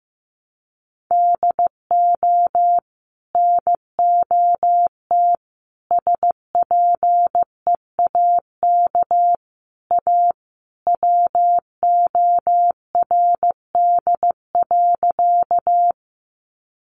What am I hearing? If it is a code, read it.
Morse code, heard as a keyed tone, about 15 wpm, 700 Hz. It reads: DO NOT SPEAK A WORD.
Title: Sentences from Top 1100 Words 15wpm